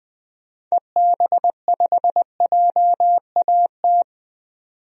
Morse code, heard as a keyed tone, about 20 wpm, 700 Hz. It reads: EB5JAT
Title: DXCC-100 Call Signs - 6 Characters - 20wpm